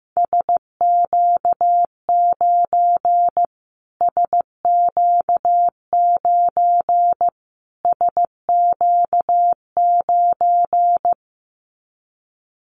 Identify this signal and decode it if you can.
Morse code, heard as a keyed tone, about 15 wpm, 700 Hz. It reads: SQ9 SQ9 SQ9
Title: Callsign Prefixes for Top 25 DXCC - Repeated 3x - 15wpm